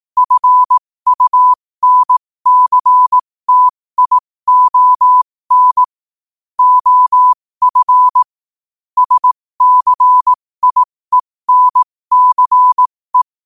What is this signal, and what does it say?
Morse code, heard as a keyed tone, about 18 wpm, spaced as for 15 wpm, 1000 Hz. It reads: FUNCTION OF SCIENCE
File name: qotdfiles / 15wpm